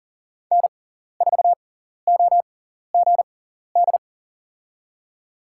Morse code, heard as a keyed tone, about 40 wpm, spaced as for 14 wpm, 700 Hz. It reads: N4OGD
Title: USA Call Signs - 5 Characters - 6x Character Spacing - 40wpm